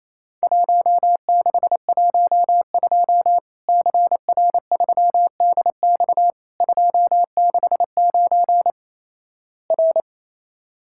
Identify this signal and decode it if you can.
Morse code, heard as a keyed tone, about 28 wpm, 700 Hz. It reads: 1612 CR3DX 269 R
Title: Worked All Europe DX Contest - QTC Exchange - 28wpm